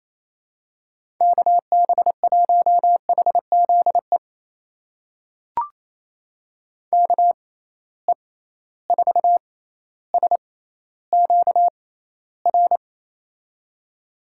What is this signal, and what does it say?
Morse code, heard as a keyed tone, about 28 wpm, spaced as for 14 wpm, 700 Hz. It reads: KB1HZE KE4SQR
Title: USA Call Signs - 6 Characters - 6x Character Spacing - 28wpm